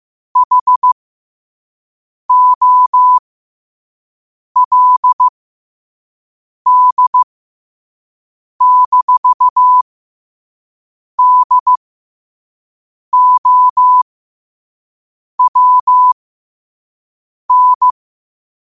Morse code, heard as a keyed tone, about 15 wpm, spaced as for 5 wpm, 1000 Hz. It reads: HOLD-DOWN
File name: qotdfiles / 5wpm